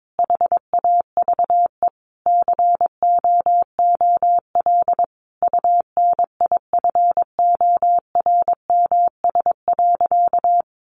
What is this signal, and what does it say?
Morse code, heard as a keyed tone, about 22 wpm, 700 Hz. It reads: HAVE COOL UNIFORMS.
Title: Easy Phrases 22wpm